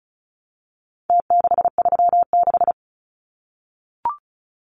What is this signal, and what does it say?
Morse code, heard as a keyed tone, about 34 wpm, 700 Hz. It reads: T636
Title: Numbers - Contest Style - 4-Digit 35wpm